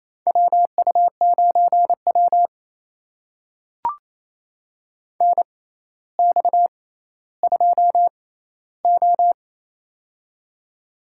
Morse code, heard as a keyed tone, about 27 wpm, spaced as for 14 wpm, 700 Hz. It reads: WU9W NX2O
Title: USA Call Signs - 4 Characters - 6x Character Spacing - 28wpm